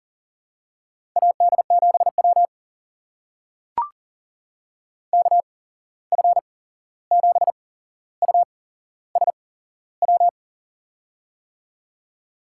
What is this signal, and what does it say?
Morse code, heard as a keyed tone, about 40 wpm, spaced as for 15 wpm, 700 Hz. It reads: AD7W KF7USW
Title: USA Call Signs - 8x Character Spacing - 40wpm